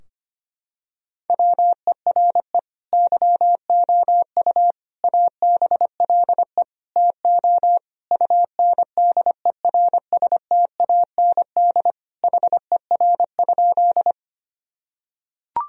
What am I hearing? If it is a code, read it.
Morse code, heard as a keyed tone, about 25 wpm, 700 Hz. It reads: WERE YOU ABLE TO UNDERSTAND HER?
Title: Sentences from Top 1000 Words - Review Part 2 - 25wpm